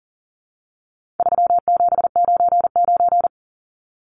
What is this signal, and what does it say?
Morse code, heard as a keyed tone, about 40 wpm, 700 Hz. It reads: 3799
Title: Numbers - 4-Digit 40wpm